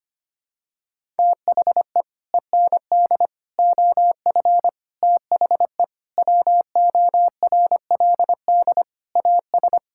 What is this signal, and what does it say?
Morse code, heard as a keyed tone, about 25 wpm, 700 Hz. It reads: THE END OF THE WORLD AS
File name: Sets of 9 Words - 25wpm